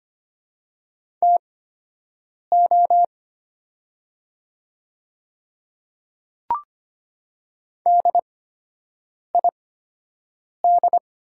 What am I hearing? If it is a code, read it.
Morse code, heard as a keyed tone, about 25 wpm, spaced as for 7 wpm, 700 Hz. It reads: TO DID
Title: 8x Character Spacing - Top 100 Words - 25wpm